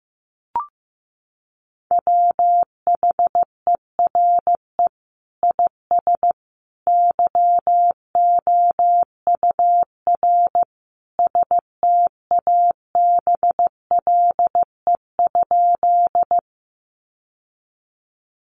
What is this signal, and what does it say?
Morse code, heard as a keyed tone, about 15 wpm, 700 Hz. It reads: WHERE IS YOUR STABLE?
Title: Sentences from Top 1500 Words 15wpm